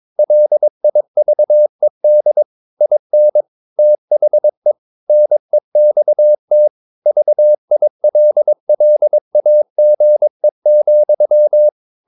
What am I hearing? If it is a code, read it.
Morse code, heard as a keyed tone, about 22 wpm, 600 Hz. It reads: LIVED IN THE NEXT VILLAGE,
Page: Romanen Never af Ken Follet på engelsk.